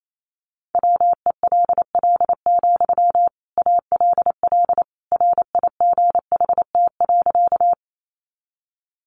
Morse code, heard as a keyed tone, about 28 wpm, 700 Hz. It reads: WELL, ALL RIGHT.
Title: Sentences from Top 200 Words 28wpm